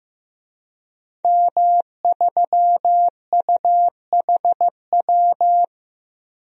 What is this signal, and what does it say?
Morse code, heard as a keyed tone, about 15 wpm, 700 Hz. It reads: M3UHW
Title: DXCC-100 Call Signs - 5 Characters - 15wpm